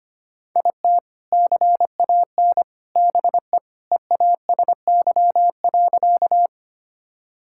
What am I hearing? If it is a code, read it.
Morse code, heard as a keyed tone, about 25 wpm, 700 Hz. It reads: IT CAN BE EASY.
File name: practice / 25wpm / Sentences with 3-Word Phrases - 25wpm